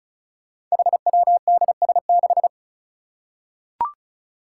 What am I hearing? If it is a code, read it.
Morse code, heard as a keyed tone, about 34 wpm, 700 Hz. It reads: HWDS6
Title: Five Challenging Letter-Number Combinations 35wpm